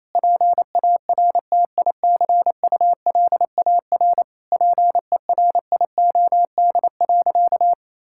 Morse code, heard as a keyed tone, about 28 wpm, 700 Hz. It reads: PARTICULAR PERIOD.
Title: Sentences from Top 1000 Words - Review Part 5 - 28wpm